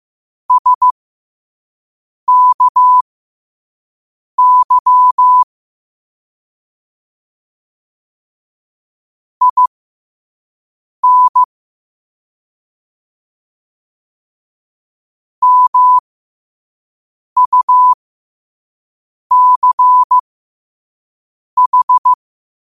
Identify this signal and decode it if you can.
Morse code, heard as a keyed tone, about 15 wpm, spaced as for 5 wpm, 1000 Hz. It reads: SKY IN MUCH
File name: qotdfiles / 5wpm